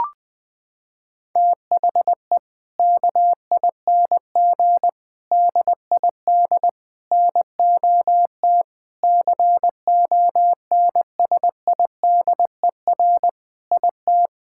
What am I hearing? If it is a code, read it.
Morse code, heard as a keyed tone, about 20 wpm, 700 Hz. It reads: THE KING DID NOT CONSIDER IT
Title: Sentences from Top 1000 Words - Review Part 5 - 20wpm